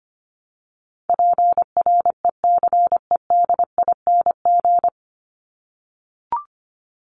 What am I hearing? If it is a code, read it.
Morse code, heard as a keyed tone, about 25 wpm, 700 Hz. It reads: PRECEDING
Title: ING Suffix - Top 100 Words with Base Words 25wpm